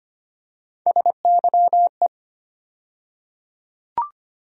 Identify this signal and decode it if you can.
Morse code, heard as a keyed tone, about 25 wpm, 700 Hz. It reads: SYE
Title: USA Call Sign Postfixes - 3 Letters 25wpm